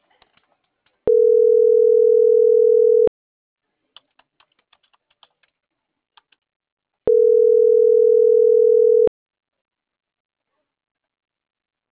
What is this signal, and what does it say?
Ringback tone (North America), 440 + 480 Hz.